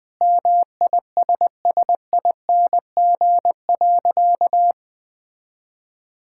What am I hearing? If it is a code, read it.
Morse code, heard as a keyed tone, about 20 wpm, 700 Hz. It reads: MISSING.
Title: Sentences with 5-Word Phrases - 20wpm